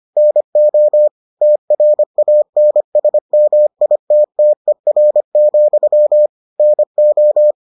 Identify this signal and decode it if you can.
Morse code, heard as a keyed tone, about 25 wpm, 600 Hz. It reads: NO TRANSMITTER, NO